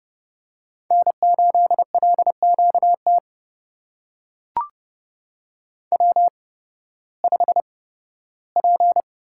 Morse code, heard as a keyed tone, about 31 wpm, spaced as for 16 wpm, 700 Hz. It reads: N8LQT W5P
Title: USA Call Signs - 8x Character Spacing - 30wpm